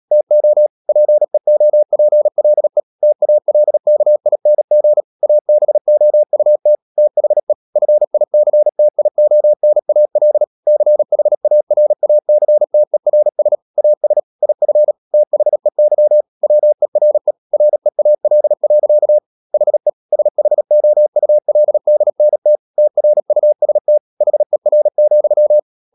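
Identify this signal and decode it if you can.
Morse code, heard as a keyed tone, about 37 wpm, 600 Hz. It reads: TO PEOPLE TALKING ABOUT THE FICTIONAL CHARACTERS AS IF THEY WERE REAL. HE SHOULDNT TRUST HER,
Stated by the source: Morse code practice at 35–39 wpm sending English text of Ken Follett's novel Never, from CW med Gnister.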